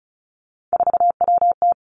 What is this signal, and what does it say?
Morse code, heard as a keyed tone, about 36 wpm, 700 Hz. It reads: 4WT
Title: Three Challenging Letter-Number Combinations 35wpm